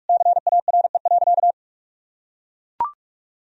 Morse code, heard as a keyed tone, about 41 wpm, 700 Hz. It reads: CARE.
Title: Sentences from Top 1000 Words - Review Part 4 - 45wpm